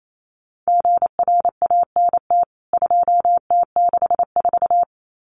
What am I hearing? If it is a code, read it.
Morse code, heard as a keyed tone, about 28 wpm, 700 Hz. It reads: GRANT 2T64
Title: CWT Exchange 28wpm